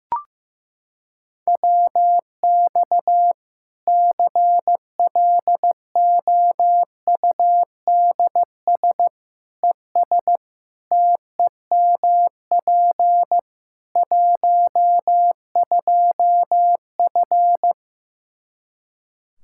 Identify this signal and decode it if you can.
Morse code, heard as a keyed tone, about 15 wpm, 700 Hz. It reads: WX CLOUDS ES TEMP 12F
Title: Practice QSO - Exchange 2 - Info - 15wpm